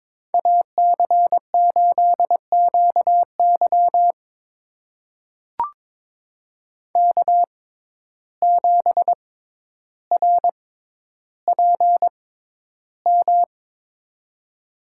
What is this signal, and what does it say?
Morse code, heard as a keyed tone, about 22 wpm, spaced as for 11 wpm, 700 Hz. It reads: AC8QY K7RPM
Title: USA Call Signs - 5 Characters - 6x Character Spacing - 22wpm